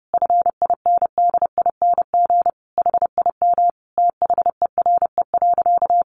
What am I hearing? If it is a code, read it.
Morse code, heard as a keyed tone, about 30 wpm, 700 Hz. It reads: FINDING HIM THERE.
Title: Sentences from Top 1100 Words 30wpm